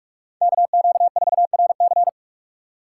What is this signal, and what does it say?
Morse code, heard as a keyed tone, about 42 wpm, 700 Hz. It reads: KQ4RC
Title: USA Call Signs - 5 Characters - 3x Character Spacing - 45wpm